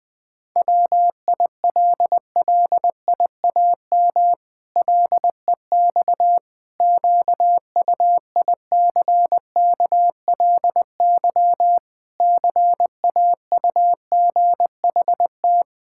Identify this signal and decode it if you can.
Morse code, heard as a keyed tone, about 20 wpm, 700 Hz. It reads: WILLIAM LEX QUICKLY CAUGHT
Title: Pangram Challenge v2 - 20wpm